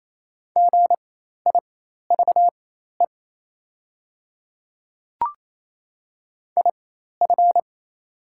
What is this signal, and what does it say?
Morse code, heard as a keyed tone, about 28 wpm, spaced as for 13 wpm, 690 Hz. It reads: GIVE IF
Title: 4x Character Spacing - Top 300 Words - 28wpm